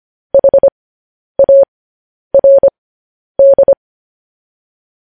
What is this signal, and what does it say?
Morse code, heard as a keyed tone, about 25 wpm, spaced as for 10 wpm, 550 Hz. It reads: HARD
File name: podcast / News headlines-1x-25x10